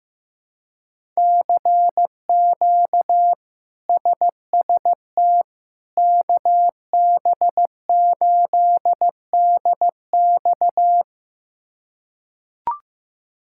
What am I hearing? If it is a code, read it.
Morse code, heard as a keyed tone, about 15 wpm, 700 Hz. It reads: CQ SST KB8DX
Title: K1USN SST QSO Practice - 15wpm